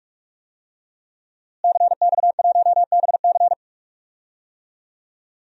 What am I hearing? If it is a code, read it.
Morse code, heard as a keyed tone, about 42 wpm, 700 Hz. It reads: CX1BC
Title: DXCC-100 Call Signs - 5 Characters - Repeated 2x - 45wpm